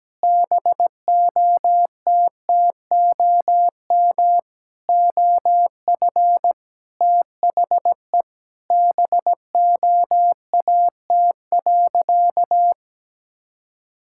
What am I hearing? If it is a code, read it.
Morse code, heard as a keyed tone, about 17 wpm, 700 Hz. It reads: BOTTOM OF THE BOAT.
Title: Sentences from Top 1000 Words - Review Part 6 - 17wpm